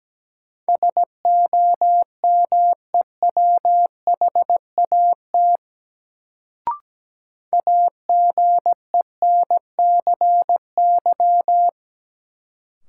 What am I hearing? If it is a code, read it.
Morse code, heard as a keyed tone, about 17 wpm, 700 Hz. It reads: SOMEWHAT AGENCY
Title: Sentences from Top 1300 Words - New Word Drill Alt - 17wpm